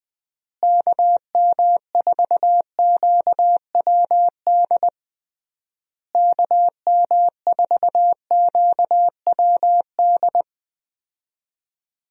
Morse code, heard as a keyed tone, about 20 wpm, 700 Hz. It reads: KM4QWD KM4QWD
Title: USA Call Signs - 6 Characters - Repeated 3x - 20wpm